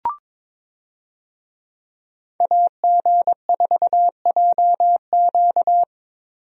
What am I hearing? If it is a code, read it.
Morse code, heard as a keyed tone, about 22 wpm, 700 Hz. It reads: AG4JQ
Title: USA Call Signs - 5 Characters - Repeated 2x - 22wpm